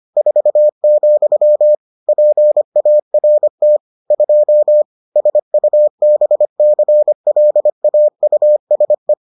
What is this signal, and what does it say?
Morse code, heard as a keyed tone, about 25 wpm, 600 Hz. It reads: 4, PART 2 SUBCLAUSE